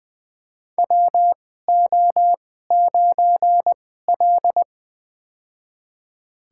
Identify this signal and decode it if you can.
Morse code, heard as a keyed tone, about 20 wpm, spaced as for 15 wpm, 700 Hz. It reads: WO9L
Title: USA Call Signs - 4 Characters - 2x Character Spacing - 20wpm